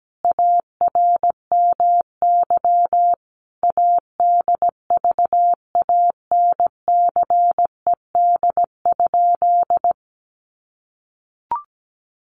Morse code, heard as a keyed tone, about 17 wpm, 700 Hz. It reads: ARMY ADVANCED?
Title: Sentences from Top 1200 Words 17wpm